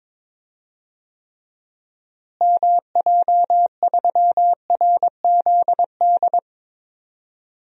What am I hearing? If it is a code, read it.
Morse code, heard as a keyed tone, about 22 wpm, 700 Hz. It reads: MJ3RZD
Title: DXCC-100 Call Signs - 6 Characters - Repeated 3x - 22wpm